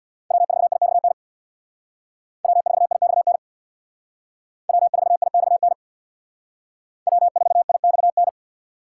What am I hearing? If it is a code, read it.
Morse code, heard as a keyed tone, about 62 wpm, 700 Hz. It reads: W4IXN W4IXN W4IXN W4IXN
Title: Speed Racing - USA 1x3 Call Signs 50wpm